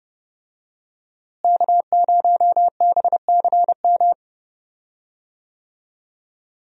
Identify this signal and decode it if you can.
Morse code, heard as a keyed tone, about 30 wpm, 700 Hz. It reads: K0BCM
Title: USA Call Signs - 5 Characters - Repeated 2x - 30wpm